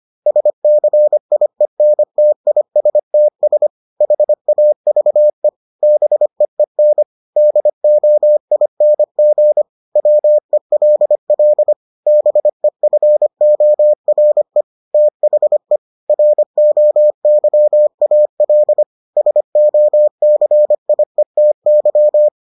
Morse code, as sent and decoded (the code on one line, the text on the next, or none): ... -.-. .. -. - .. ... - ... / .... .- ...- . / -... . -. / -.. --- .. -. --. / .-- . .-.. .-.. / -... ..-. --- .-. / - .... / .-. --- -.-- .- .-.. / ... --- -.-. .. - -.--
SCIENTISTS HAVE BEEN DOING WELL BEFORE THE ROYAL SOCIETY